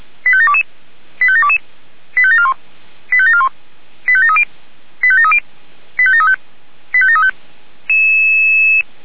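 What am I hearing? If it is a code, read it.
Five-tone selective call (ZVEI), heard as quick runs of five tones, tones of about 70 ms. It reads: 85722 85722 85741 85741 85730 85730 85735 85735
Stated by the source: Signal received via SDR, ZVEI five-tone.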